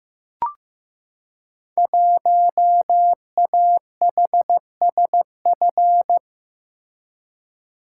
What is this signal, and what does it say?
Morse code, heard as a keyed tone, about 15 wpm, 700 Hz. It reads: 1AHSF
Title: Code Groups - 5 Letter:Numbers - 15wpm